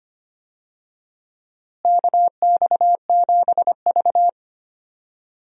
Morse code, heard as a keyed tone, about 25 wpm, 700 Hz. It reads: KX7V
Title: USA Call Signs - 4 Characters - Repeated 3x - 25wpm